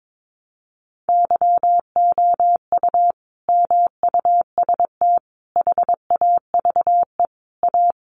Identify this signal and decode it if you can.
Morse code, heard as a keyed tone, about 22 wpm, 700 Hz. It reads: YOU MUST HAVE A
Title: Sentences from Top 500 Words 22wpm